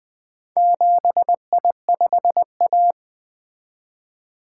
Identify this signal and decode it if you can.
Morse code, heard as a keyed tone, about 25 wpm, 700 Hz. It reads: 7I5A
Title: Speed-Racing - Four Letter-Number Combinations - 20wpm